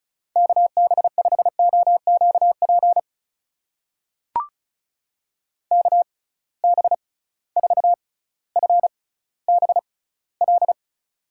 Morse code, heard as a keyed tone, about 36 wpm, spaced as for 18 wpm, 700 Hz. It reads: KB5OQP KB4FBL
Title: USA Call Signs - 6 Characters - 6x Character Spacing - 35wpm